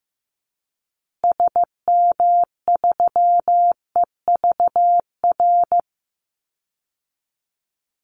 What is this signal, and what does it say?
Morse code, heard as a keyed tone, about 14 wpm, 700 Hz. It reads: SM3EVR
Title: Call Signs Repeated 2x - 15wpm